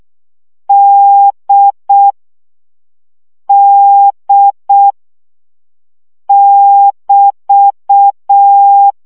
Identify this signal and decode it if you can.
Morse code, heard as a keyed tone, about 6 wpm, 800 Hz. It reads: D D <BT>
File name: CW gyakorlas d 5 baud